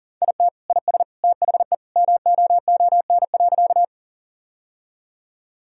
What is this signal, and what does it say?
Morse code, heard as a keyed tone, about 40 wpm, 700 Hz. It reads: IT IS THE MOON.
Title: Sentences from Top 1000 Words - Review Part 3 - 40wpm